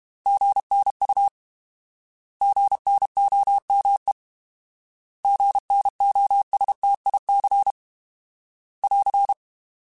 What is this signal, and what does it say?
Morse code, heard as a keyed tone, about 32 wpm, 800 Hz. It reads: GNU GNOME GNOSTIC <AR>